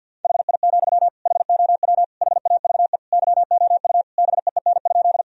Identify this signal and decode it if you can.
Morse code, heard as a keyed tone, about 51 wpm, 700 Hz. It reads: HI, HOW HAVE YOU BEEN?
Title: Sentences from Top 1000 Words - Review Part 3 - 50wpm